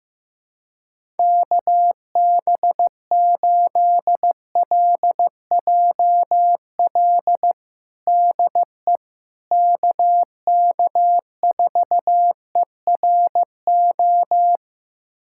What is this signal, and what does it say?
Morse code, heard as a keyed tone, about 15 wpm, 700 Hz. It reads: KB8LJL DE KK4ERO
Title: USA Callsign DE Callsign 15wpm